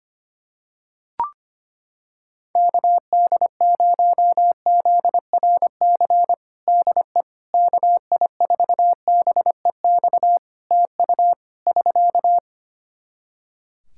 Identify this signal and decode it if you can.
Morse code, heard as a keyed tone, about 25 wpm, 700 Hz. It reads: KD0ZRC DE KI4BEX TU <SK>
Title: Practice QSO - Exchange 4 - Ending - 25wpm